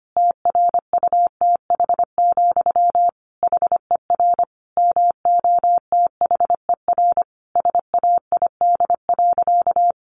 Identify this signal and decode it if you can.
Morse code, heard as a keyed tone, about 25 wpm, 700 Hz. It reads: TRUTH, HER MOTHER SAID.
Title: Sentences from Top 800 Words 25wpm